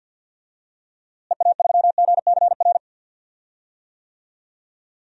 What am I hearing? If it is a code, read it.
Morse code, heard as a keyed tone, about 46 wpm, 700 Hz. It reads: EA3GCR
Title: DXCC-100 Call Signs - 6 Characters - Repeated 3x - 50wpm